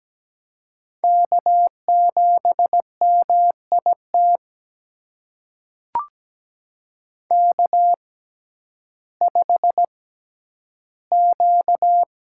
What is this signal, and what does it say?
Morse code, heard as a keyed tone, about 17 wpm, spaced as for 10 wpm, 700 Hz. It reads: K7MIT K5Q
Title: USA Call Signs - 6x Character Spacing - 17wpm